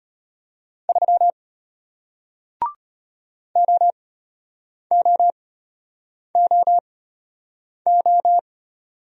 Morse code, heard as a keyed tone, about 33 wpm, spaced as for 8 wpm, 700 Hz. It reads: 3 OOOO